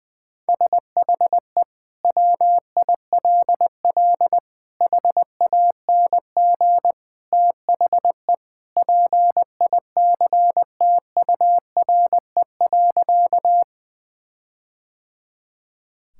Morse code, heard as a keyed tone, about 20 wpm, 700 Hz. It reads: SHE WILL HANG THE PICTURE.
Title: Sentences from Top 1600 Words 20wpm